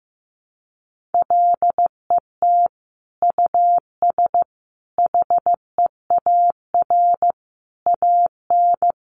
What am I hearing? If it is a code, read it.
Morse code, heard as a keyed tone, about 15 wpm, 700 Hz. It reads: LET US HEAR AN
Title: Sentences from Top 900 Words 15wpm